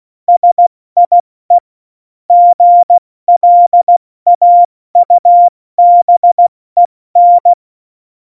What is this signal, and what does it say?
Morse code, heard as a keyed tone, about 16 wpm, spaced as for 14 wpm, 700 Hz. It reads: SIE GLAUBEN